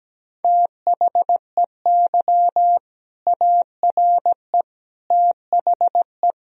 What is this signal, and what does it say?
Morse code, heard as a keyed tone, about 17 wpm, 700 Hz. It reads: THEY ARE THE
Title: Sentences from Top 1000 Words - Review Part 1 - 17wpm